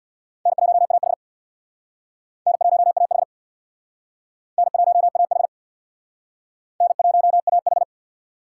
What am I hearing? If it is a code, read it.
Morse code, heard as a keyed tone, about 60 wpm, 700 Hz. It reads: N1AH N1AH N1AH N1AH
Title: Speed Racing - USA 1x2 Call Signs 50wpm